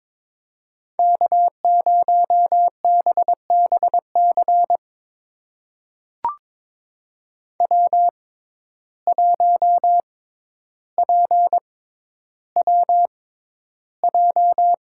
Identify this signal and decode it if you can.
Morse code, heard as a keyed tone, about 22 wpm, spaced as for 11 wpm, 700 Hz. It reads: K0BBC W1PWJ
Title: USA Call Signs - 5 Characters - 6x Character Spacing - 22wpm